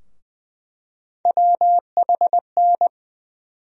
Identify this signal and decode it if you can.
Morse code, heard as a keyed tone, about 20 wpm, 700 Hz. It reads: WHN
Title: USA Call Sign Postfixes - 3 Letters 20wpm